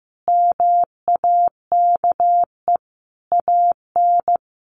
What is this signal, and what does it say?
Morse code, heard as a keyed tone, about 15 wpm, 700 Hz. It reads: MAKE AN
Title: Sentences from Top 1200 Words 15wpm